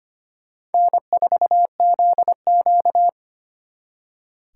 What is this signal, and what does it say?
Morse code, heard as a keyed tone, about 32 wpm, 700 Hz. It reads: N4ZQ